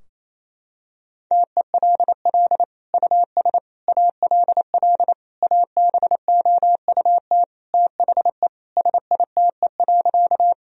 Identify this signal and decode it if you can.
Morse code, heard as a keyed tone, about 28 wpm, 700 Hz. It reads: TELL US ALL ABOUT THE SITE.
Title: Sentences from Top 1000 Words - Review Part 4 - 28wpm